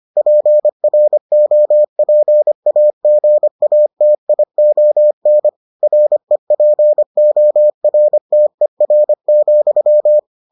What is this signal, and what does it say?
Morse code, heard as a keyed tone, about 25 wpm, 600 Hz. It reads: PROPAGATION REPORTER,